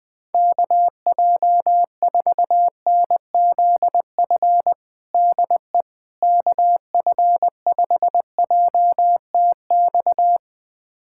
Morse code, heard as a keyed tone, about 20 wpm, 700 Hz. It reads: KJ4NZF DE KF5JTX
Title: USA Callsign DE Callsign - 6 Characters - 20wpm